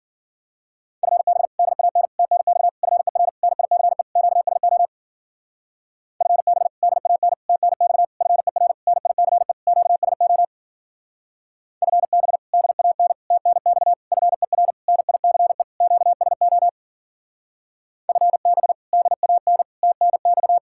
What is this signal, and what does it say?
Morse code, heard as a keyed tone, about 49 wpm, 700 Hz. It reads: FB DAN TNX FER NICE QSO FB DAN TNX FER NICE QSO FB DAN TNX FER NICE QSO FB DAN TNX
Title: Speed-Racing Common QSO Phrases - 40wpm